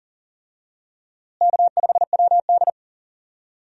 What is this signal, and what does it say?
Morse code, heard as a keyed tone, about 37 wpm, 700 Hz. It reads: K5WD
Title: USA Call Signs - 4 Characters - Repeated 3x - 40wpm